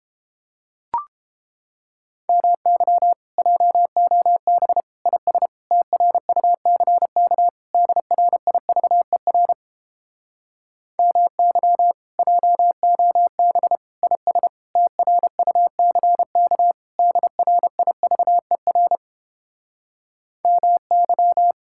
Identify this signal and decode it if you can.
Morse code, heard as a keyed tone, about 29 wpm, 700 Hz. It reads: MY JOB IS TRUCK DRIVER MY JOB IS TRUCK DRIVER MY